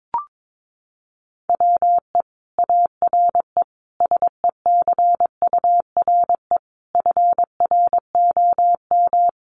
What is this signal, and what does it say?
Morse code, heard as a keyed tone, about 22 wpm, 700 Hz. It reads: WE ARE SECURE FROM
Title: Sentences from Top 1200 Words 22wpm